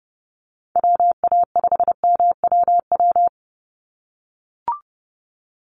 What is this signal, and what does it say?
Morse code, heard as a keyed tone, about 30 wpm, 700 Hz. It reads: WA5MWW
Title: USA Call Signs 30wpm